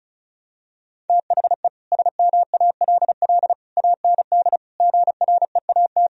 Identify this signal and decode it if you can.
Morse code, heard as a keyed tone, about 36 wpm, 700 Hz. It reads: THE SMALL AND GREAT
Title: Sentences from Top 1000 Words - Review Part 1 - 35wpm